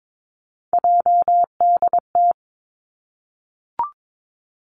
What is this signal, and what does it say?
Morse code, heard as a keyed tone, about 22 wpm, 700 Hz. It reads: JDT
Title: Three Challenging Letter-Number Combinations 22wpm